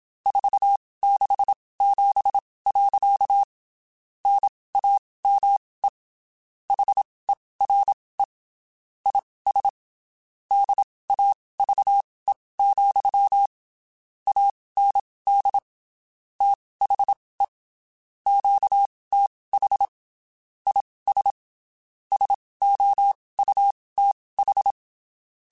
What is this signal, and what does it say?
Morse code, heard as a keyed tone, about 27 wpm, spaced as for 19 wpm, 800 Hz. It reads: 467. NAME HERE IS DAVE, AND THE QTH IS SOUTH
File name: QSOs Spaced